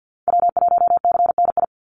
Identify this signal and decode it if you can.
Morse code, heard as a keyed tone, about 50 wpm, 700 Hz. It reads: W1CNI
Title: USA Call Signs 50wpm